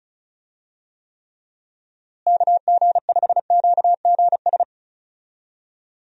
Morse code, heard as a keyed tone, about 36 wpm, 700 Hz. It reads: KG5QGS
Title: USA Call Signs - 6 Characters - Repeated 3x - 35wpm